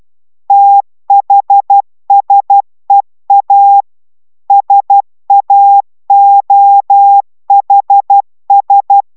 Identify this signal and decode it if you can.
Morse code, heard as a keyed tone, about 12 wpm, 800 Hz. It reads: THSEA SAOHS
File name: taviro tanulas / cw 69 04 10